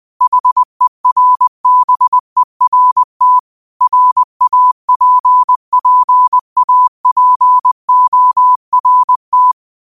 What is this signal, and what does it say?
Morse code, heard as a keyed tone, about 20 wpm, 1000 Hz. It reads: HERBERT RAPPAPORT
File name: qotdfiles / 20wpm